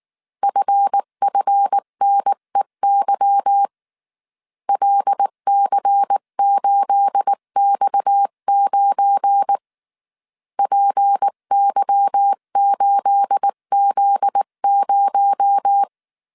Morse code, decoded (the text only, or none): FFNEY LC8X9 PY8Z0